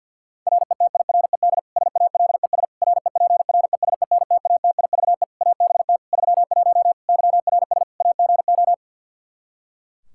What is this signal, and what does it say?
Morse code, heard as a keyed tone, about 51 wpm, 700 Hz. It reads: RETIRED SALES REPRESENTATIVE ABT 31 YRS AGO
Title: Practice QSO - Exchange 3 - More Info - 50wpm